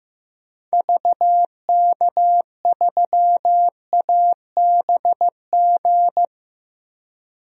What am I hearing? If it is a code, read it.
Morse code, heard as a keyed tone, about 15 wpm, 700 Hz. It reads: VK3ABG
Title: DXCC-100 Call Signs - 6 Characters - 15wpm